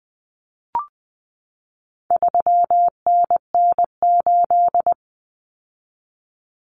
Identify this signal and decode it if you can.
Morse code, heard as a keyed tone, about 20 wpm, 700 Hz. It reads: 3NN8
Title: Common Cut Numbers - 4-Digit 20wpm